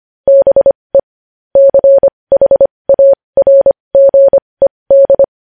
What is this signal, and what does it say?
Morse code, heard as a keyed tone, about 25 wpm, spaced as for 20 wpm, 550 Hz. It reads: BE CHARGED
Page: news headlines in morse code at 20 wpm